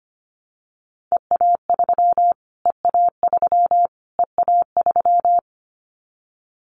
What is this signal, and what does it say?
Morse code, heard as a keyed tone, about 25 wpm, 700 Hz. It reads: EA3 EA3 EA3
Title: Callsign Prefixes for Top 25 DXCC - Repeated 3x - 25wpm